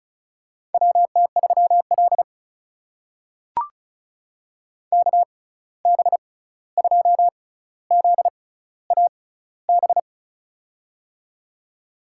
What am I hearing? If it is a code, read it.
Morse code, heard as a keyed tone, about 36 wpm, spaced as for 16 wpm, 700 Hz. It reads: WT3L KB2ZAB
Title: USA Call Signs - 6x Character Spacing - 35wpm